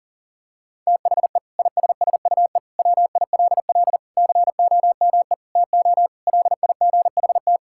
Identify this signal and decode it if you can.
Morse code, heard as a keyed tone, about 40 wpm, 700 Hz. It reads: THE ISSUE WILL COME TO LIGHT
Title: Sentences from Top 1000 Words - Review Part 6 - 40wpm